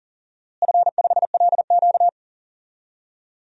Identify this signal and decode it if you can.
Morse code, heard as a keyed tone, about 40 wpm, 700 Hz. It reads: F5LQ
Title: DXCC-100 Call Signs - 4 Characters - Repeated 3x - 40wpm